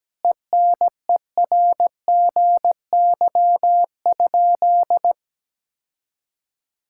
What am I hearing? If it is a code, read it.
Morse code, heard as a keyed tone, about 17 wpm, 700 Hz. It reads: ENERGY?
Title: Sentences from Top 1000 Words - Review Part 6 - 17wpm